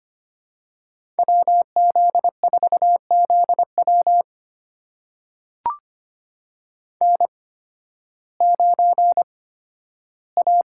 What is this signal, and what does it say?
Morse code, heard as a keyed tone, about 25 wpm, spaced as for 13 wpm, 700 Hz. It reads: WZ4ZW N9A
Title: USA Call Signs - 5 Characters - 8x Character Spacing - 25wpm